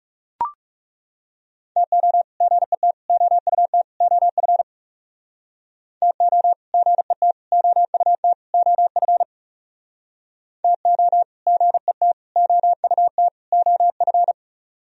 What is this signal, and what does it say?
Morse code, heard as a keyed tone, about 38 wpm, 700 Hz. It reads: TO GET OUT OF TO GET OUT OF TO GET OUT OF